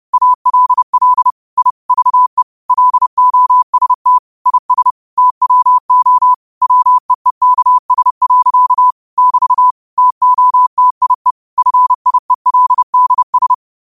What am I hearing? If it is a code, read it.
Morse code, heard as a keyed tone, about 30 wpm, 1000 Hz. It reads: ALL I VE LOST IS TWO WEEKS. <BT> TOTIE FIELDS
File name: qotdfiles / 30wpm